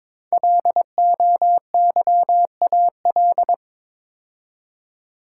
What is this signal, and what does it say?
Morse code, heard as a keyed tone, about 22 wpm, 700 Hz. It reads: LOYAL
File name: 5 Letter Words - First 3 Letters Given - 22wpm